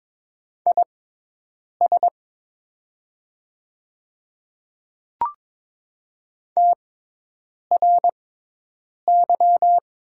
Morse code, heard as a keyed tone, about 22 wpm, spaced as for 8 wpm, 700 Hz. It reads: IS TRY
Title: 6x Character Spacing - Top 200 Words - 22wpm